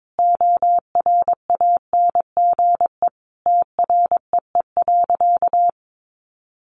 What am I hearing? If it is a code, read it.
Morse code, heard as a keyed tone, about 22 wpm, 700 Hz. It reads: ORANGE TREE.
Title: Easy Phrases 22wpm